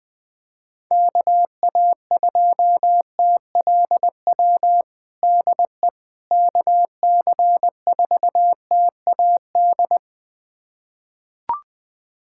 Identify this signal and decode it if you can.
Morse code, heard as a keyed tone, about 21 wpm, 700 Hz. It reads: KA2TLW DE KC4TAD
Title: USA Callsign DE Callsign - 6 Characters - 20wpm